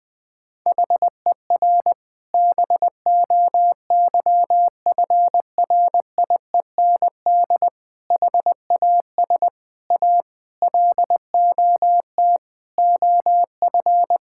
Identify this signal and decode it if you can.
Morse code, heard as a keyed tone, about 20 wpm, 700 Hz. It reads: HER BOYFRIEND HAS A LOT OF
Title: Sentences from Top 1000 Words - Review Part 5 - 20wpm